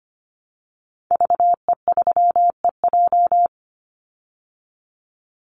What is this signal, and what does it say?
Morse code, heard as a keyed tone, about 25 wpm, 700 Hz. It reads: VE3EJ
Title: Call Signs Repeated 2x - 25wpm